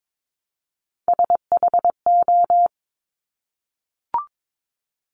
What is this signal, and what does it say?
Morse code, heard as a keyed tone, about 22 wpm, 700 Hz. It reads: SHO
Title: Top Three Letter Combinations in Words 22wpm